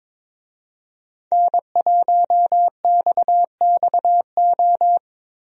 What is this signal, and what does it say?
Morse code, heard as a keyed tone, about 22 wpm, 700 Hz. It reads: N1XXO
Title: USA Call Signs - 5 Characters - 22wpm